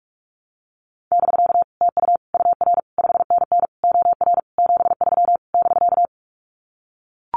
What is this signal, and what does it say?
Morse code, heard as a keyed tone, about 44 wpm, 700 Hz. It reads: <BK> TU UR 5NN OR 73 <BK>
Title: Common QSO Phrases Encore 45wpm